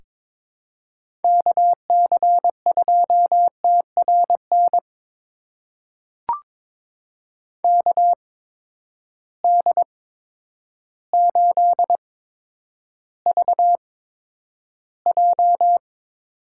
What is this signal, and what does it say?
Morse code, heard as a keyed tone, about 21 wpm, spaced as for 10 wpm, 700 Hz. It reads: KC2TRN KD8VJ
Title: USA Call Signs - 6 Characters - 8x Character Spacing - 22wpm